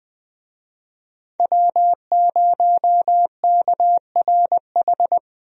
Morse code, heard as a keyed tone, about 20 wpm, 700 Hz. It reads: W0KRH
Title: USA Call Signs - 5 Characters - Repeated 2x - 20wpm